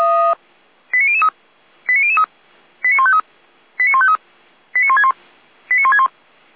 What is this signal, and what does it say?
Five-tone selective call (ZVEI), heard as quick runs of five tones, tones of about 70 ms. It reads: 89003 89003 89153 89153 89161 89161
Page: SDR/ZVEI